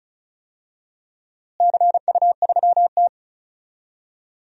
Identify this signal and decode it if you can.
Morse code, heard as a keyed tone, about 34 wpm, 700 Hz. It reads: CU3T
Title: DXCC-100 Call Signs - 4 Characters - Repeated 2x - 35wpm